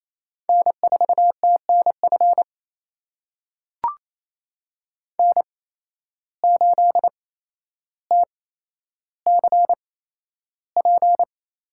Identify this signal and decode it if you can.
Morse code, heard as a keyed tone, about 28 wpm, spaced as for 12 wpm, 700 Hz. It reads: N4TNF N8TCP
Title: USA Call Signs - 5 Characters - 8x Character Spacing - 28wpm